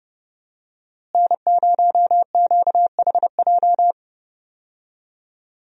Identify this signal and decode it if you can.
Morse code, heard as a keyed tone, about 30 wpm, 700 Hz. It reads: N0QHJ
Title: USA Call Signs - 5 Characters - Repeated 2x - 30wpm